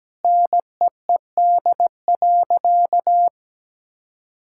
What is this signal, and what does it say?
Morse code, heard as a keyed tone, about 17 wpm, 700 Hz. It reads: NEED.
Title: Sentences from Top 1000 Words - Review Part 5 - 17wpm